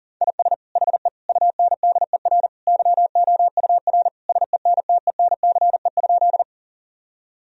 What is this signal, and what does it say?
Morse code, heard as a keyed tone, about 40 wpm, 700 Hz. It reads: IS HE UNDER YOUR SENTENCE?
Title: Sentences from Top 1000 Words - Review Part 1 - 40wpm